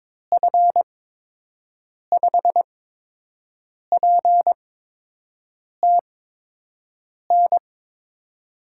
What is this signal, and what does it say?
Morse code, heard as a keyed tone, about 22 wpm, spaced as for 6 wpm, 700 Hz. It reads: F5PTN
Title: USA Call Signs - 6 Characters - 8x Character Spacing - 22wpm